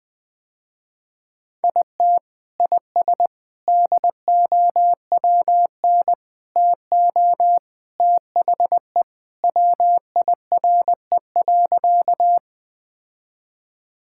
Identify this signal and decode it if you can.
Morse code, heard as a keyed tone, about 20 wpm, 700 Hz. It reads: IT IS DOWN TO THE WIRE.
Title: Sentences with 3-Word Phrases - Spoken Beginning - 20wpm